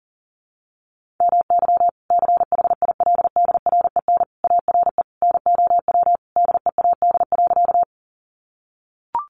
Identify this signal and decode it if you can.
Morse code, heard as a keyed tone, about 40 wpm, 700 Hz. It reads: MY CHILDREN ARE NOW DEAD.
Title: Sentences from Top 1500 Words 40wpm